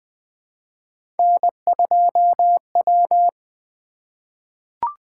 Morse code, heard as a keyed tone, about 20 wpm, 700 Hz. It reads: N2W
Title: Three Letter-Number Combinations - 20wpm